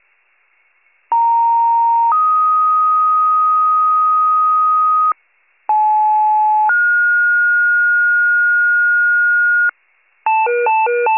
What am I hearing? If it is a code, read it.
Two-tone page, heard as a one-second tone, then a three-second tone, about 930 then 1250 Hz, about 860 then 1430 Hz, with noise. It is heard from 1.1 to 9.7 s.